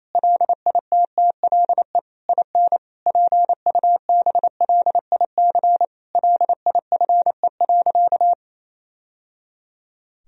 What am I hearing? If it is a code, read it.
Morse code, heard as a keyed tone, about 28 wpm, 700 Hz. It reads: LITTLE IN PUBLIC LIFE.
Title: Sentences from Top 1000 Words - Review Part 3 - 28wpm